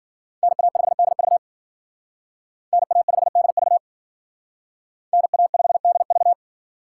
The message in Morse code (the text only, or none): NA5DV NA5DV NA5DV